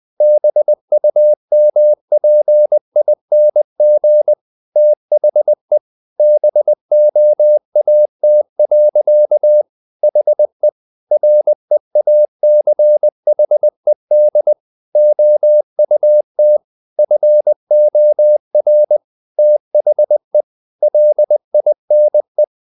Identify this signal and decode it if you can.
Morse code, heard as a keyed tone, about 20 wpm, 600 Hz. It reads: BUMPING THE BOAT. HE REACHED OUT FOR THE LINE
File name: the old man and the sea / the old man and the sea part 1 20wpm